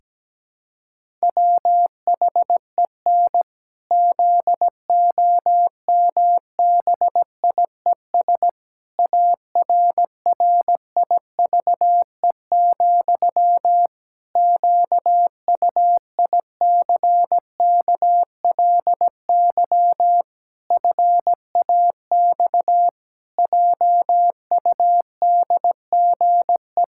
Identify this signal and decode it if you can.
Morse code, heard as a keyed tone, about 17 wpm, 700 Hz. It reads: WHEN ZOMBIES ARRIVE, QUICKLY FAX JUDGE
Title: Pangram Challenge v2 - 17wpm